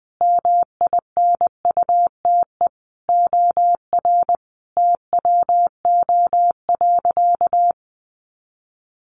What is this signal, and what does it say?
Morse code, heard as a keyed tone, about 20 wpm, 700 Hz. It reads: MINUTE OR TWO.
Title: Sentences from Top 1400 Words 20wpm